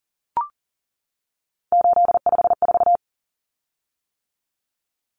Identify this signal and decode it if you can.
Morse code, heard as a keyed tone, about 40 wpm, 700 Hz. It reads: 854
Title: Common Cut Numbers - 3-Digit 40wpm